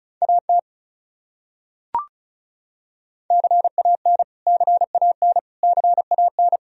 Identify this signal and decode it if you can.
Morse code, heard as a keyed tone, about 36 wpm, 700 Hz. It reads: AT CAN CAN CAN
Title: Top 100 Words Repeated 35wpm